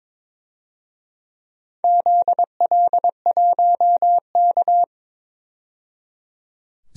Morse code, heard as a keyed tone, about 23 wpm, 700 Hz. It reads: ZL1K
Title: DXCC-100 Call Signs - 4 Characters - 22wpm